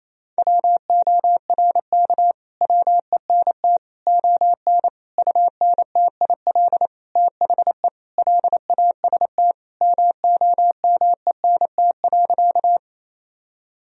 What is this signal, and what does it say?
Morse code, heard as a keyed tone, about 28 wpm, 700 Hz. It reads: WORK WENT ON UNTIL THE LAST MOMENT.
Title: Sentences from Top 1000 Words - Review Part 1 - 28wpm